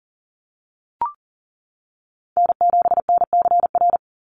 Morse code, heard as a keyed tone, about 40 wpm, 700 Hz. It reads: N7NCR
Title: USA Call Signs 40wpm